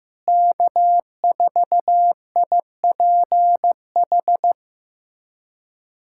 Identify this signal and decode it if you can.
Morse code, heard as a keyed tone, about 15 wpm, 700 Hz. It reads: K4IPH
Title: USA Callsign DE Callsign 15wpm